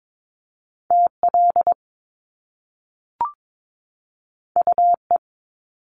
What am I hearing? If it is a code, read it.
Morse code, heard as a keyed tone, about 22 wpm, 700 Hz. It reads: TL UE
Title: Top Two Letter Combinations in Words 22wpm